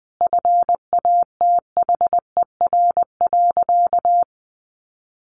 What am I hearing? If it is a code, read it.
Morse code, heard as a keyed tone, about 20 wpm, 700 Hz. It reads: FATHER.
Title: Sentences from Top 1000 Words 20wpm